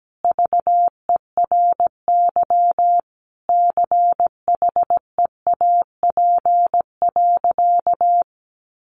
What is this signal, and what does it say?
Morse code, heard as a keyed tone, about 17 wpm, 700 Hz. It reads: VERY CHEAP.
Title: Sentences from Top 1200 Words 17wpm